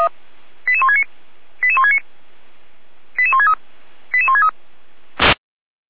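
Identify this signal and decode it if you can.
Five-tone selective call (ZVEI), heard as quick runs of five tones, tones of about 70 ms. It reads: 88169 88169 88163 88163
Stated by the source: Five-tone ZVEI selective call 88551, received on SDR.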